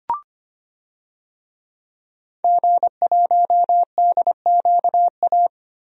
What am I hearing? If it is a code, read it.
Morse code, heard as a keyed tone, about 25 wpm, 700 Hz. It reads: G1DQA
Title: DXCC-100 Call Signs - 5 Characters - Repeated 2x - 25wpm